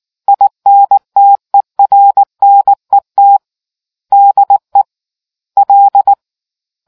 Morse code, heard as a keyed tone, about 19 wpm, 800 Hz. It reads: INTERNET DE L